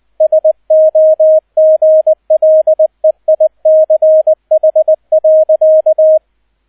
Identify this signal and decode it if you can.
Morse code, heard as a keyed tone, about 20 wpm, 630 Hz, with radio noise. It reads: SOGLEICH.
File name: Erster Streich 20wpm